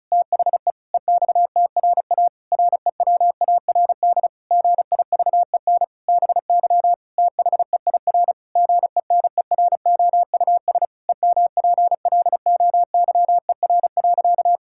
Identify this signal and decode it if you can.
Morse code, heard as a keyed tone, about 36 wpm, 700 Hz. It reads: THE EXTRA REWARD GIVEN BY THEIR GENEROUS EMPLOYER.
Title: Pangram Challenge v2 - 35wpm